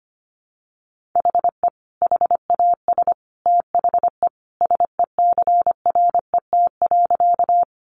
Morse code, heard as a keyed tone, about 25 wpm, 700 Hz. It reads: HE HAS THE SECRET.
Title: Sentences from Top 1200 Words 25wpm